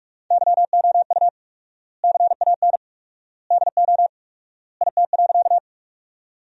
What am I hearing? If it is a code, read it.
Morse code, heard as a keyed tone, about 42 wpm, 700 Hz. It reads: YOU CAN DO IT.
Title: 4x Word-Spacing - Sentences from Top 100 Words - 45wpm